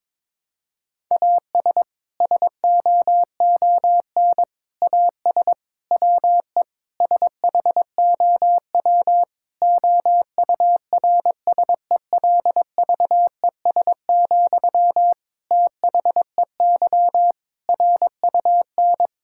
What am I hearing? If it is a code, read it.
Morse code, heard as a keyed tone, about 22 wpm, 700 Hz. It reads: AS SOON AS WE SHOW OURSELVES, THEY RUN
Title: Sentences from Top 1600 Words 22wpm